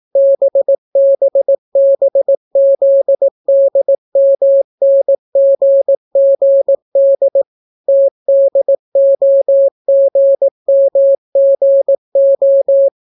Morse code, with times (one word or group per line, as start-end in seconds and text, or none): 0.2-7.4 BBBZDMNGGD
7.9-12.9 TDOGMGO